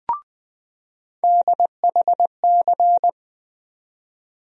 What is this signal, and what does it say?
Morse code, heard as a keyed tone, about 20 wpm, 700 Hz. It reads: DHC
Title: USA Call Sign Postfixes - 3 Letters 20wpm